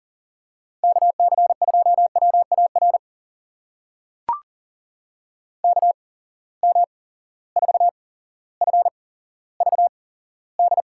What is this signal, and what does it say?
Morse code, heard as a keyed tone, about 38 wpm, spaced as for 17 wpm, 700 Hz. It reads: KC2WAR KM4FVD
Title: USA Call Signs - 6 Characters - 8x Character Spacing - 40wpm